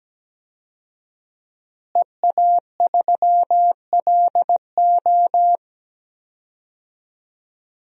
Morse code, heard as a keyed tone, about 17 wpm, 700 Hz. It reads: EA3LO
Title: DXCC-100 Call Signs - 5 Characters - Repeated 3x - 17wpm